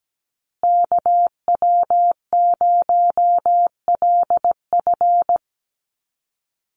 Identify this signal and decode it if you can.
Morse code, heard as a keyed tone, about 17 wpm, 700 Hz. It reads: KW0LF
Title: USA Call Signs Repeated 3x 17wpm